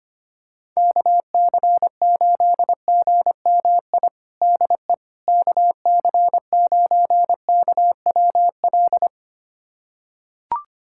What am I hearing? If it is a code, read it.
Morse code, heard as a keyed tone, about 25 wpm, 700 Hz. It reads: KC8GMI DE KC9KWL
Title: USA Callsign DE Callsign - 6 Characters - 25wpm